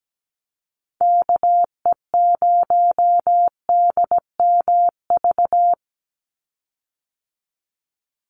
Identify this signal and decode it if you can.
Morse code, heard as a keyed tone, about 17 wpm, 700 Hz. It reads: KE0DMV
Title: USA Call Signs 17wpm